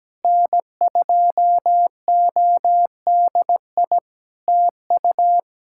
Morse code, heard as a keyed tone, about 17 wpm, 700 Hz. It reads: N2ODI TU
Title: K1USN SST QSO Practice 17wpm